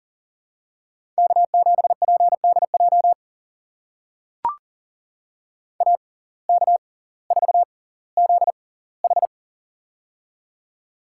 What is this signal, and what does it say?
Morse code, heard as a keyed tone, about 40 wpm, spaced as for 20 wpm, 700 Hz. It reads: K7PDJ AK4ZH
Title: USA Call Signs - 5 Characters - 6x Character Spacing - 40wpm